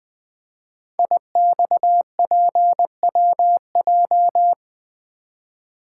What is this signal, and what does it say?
Morse code, heard as a keyed tone, about 20 wpm, 700 Hz. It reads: IXPWJ
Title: Code Groups - 5 Letters - 20wpm